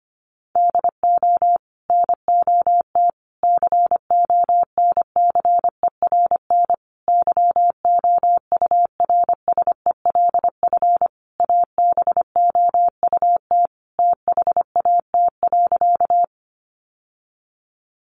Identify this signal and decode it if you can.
Morse code, heard as a keyed tone, about 25 wpm, 700 Hz. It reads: DO NOT CONCERN YOURSELF ABOUT THAT.
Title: Sentences from Top 1000 Words 25wpm